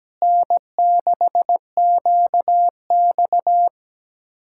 Morse code, heard as a keyed tone, about 17 wpm, 700 Hz. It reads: N6QX
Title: Speed-Racing Common QSO Phrases - 17wpm